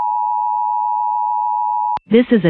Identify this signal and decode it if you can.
Emergency Alert System (SAME) tones (the 853 and 960 Hz attention tone), with speech. It is heard from 0.0 to 2.0 s.